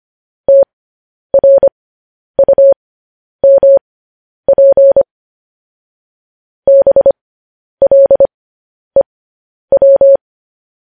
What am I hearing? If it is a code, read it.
Morse code, heard as a keyed tone, about 25 wpm, spaced as for 10 wpm, 550 Hz. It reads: TRUMP BLEW
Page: news-headlines-in-morse-code-at-10-wpm